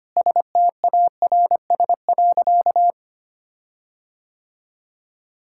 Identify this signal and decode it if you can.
Morse code, heard as a keyed tone, about 25 wpm, 700 Hz. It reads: STARS.
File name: practice / 25wpm / Difficult Sentences with Hints - 25wpm